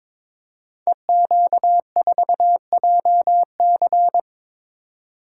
Morse code, heard as a keyed tone, about 22 wpm, 700 Hz. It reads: EQ4JC
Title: Code Groups - 5 Letter:Numbers - 22wpm